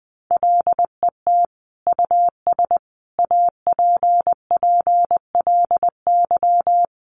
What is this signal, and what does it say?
Morse code, heard as a keyed tone, about 20 wpm, 700 Hz. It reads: LET US APPLY
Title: Sentences from Top 1200 Words 20wpm